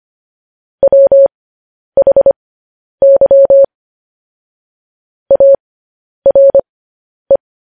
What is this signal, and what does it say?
Morse code, heard as a keyed tone, about 25 wpm, spaced as for 10 wpm, 550 Hz. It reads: WHY ARE
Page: news-headlines-in-morse-code-at-10-wpm